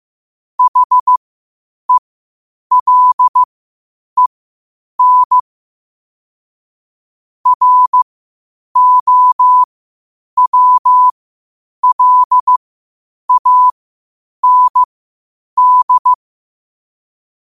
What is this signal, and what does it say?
Morse code, heard as a keyed tone, about 15 wpm, spaced as for 8 wpm, 1000 Hz. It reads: HELEN ROWLAND
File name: qotdfiles / 8wpm